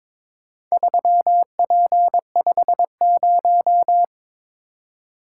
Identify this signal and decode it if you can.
Morse code, heard as a keyed tone, about 22 wpm, 690 Hz. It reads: 3P50
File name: practice / 22wpm / Four Letter-Number Combinations - 22wpm